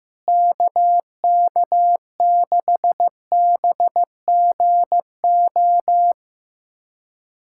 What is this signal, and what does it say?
Morse code, heard as a keyed tone, about 16 wpm, 700 Hz. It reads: KK6BGO
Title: USA Callsign DE Callsign - 6 Characters - 15wpm